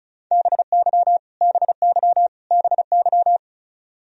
Morse code, heard as a keyed tone, about 34 wpm, 700 Hz. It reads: BY BY BY
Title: Top 100 Words Repeated 35wpm